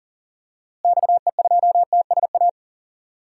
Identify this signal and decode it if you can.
Morse code, heard as a keyed tone, about 40 wpm, 700 Hz. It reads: XE2TSA
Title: DXCC-100 Call Signs - 6 Characters - Repeated 2x - 40wpm